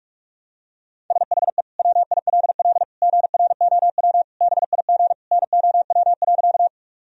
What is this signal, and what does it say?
Morse code, heard as a keyed tone, about 41 wpm, 700 Hz. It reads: SHE WILL GROW BIG NOW.
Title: Sentences from Top 1000 Words - Review Part 1 - 45wpm